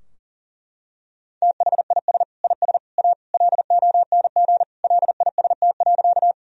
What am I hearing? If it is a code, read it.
Morse code, heard as a keyed tone, about 40 wpm, 700 Hz. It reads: THIS IS A LONG LIST.
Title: Sentences from Top 1000 Words - Review Part 2 - 40wpm